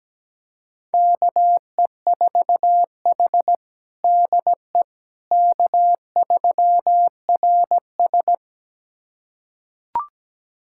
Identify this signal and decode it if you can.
Morse code, heard as a keyed tone, about 17 wpm, 700 Hz. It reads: KE4H DE K3RS
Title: USA Callsign DE Callsign - 4 Characters - 17wpm